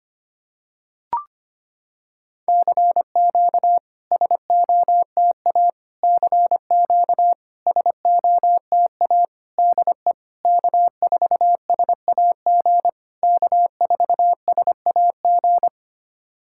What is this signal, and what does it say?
Morse code, heard as a keyed tone, about 25 wpm, 700 Hz. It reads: CQ SOTA CQ SOTA DE K4SAG K4SAG
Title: SOTA QSO Practice 25wpm